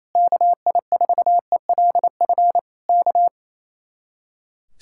Morse code, heard as a keyed tone, about 31 wpm, 700 Hz. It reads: KI4ELF K
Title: Speed-Racing Common QSO Phrases Encore - 28wpm